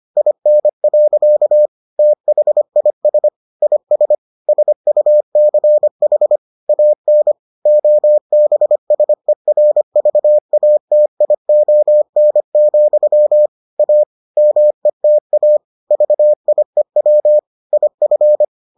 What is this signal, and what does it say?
Morse code, heard as a keyed tone, about 25 wpm, 600 Hz. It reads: IN. THIS IS SUCH AN OBSERVATION, A META VIEW IF